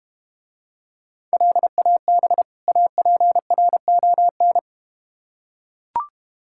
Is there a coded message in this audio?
Yes: Morse code from 1.3 to 4.6 s, heard as a keyed tone, about 31 wpm, 700 Hz. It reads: LAB APRON